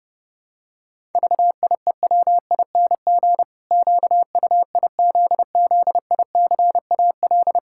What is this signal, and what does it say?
Morse code, heard as a keyed tone, about 30 wpm, 700 Hz. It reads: VIEWING QUIZZICAL
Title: Pangram Challenge v2 - 30wpm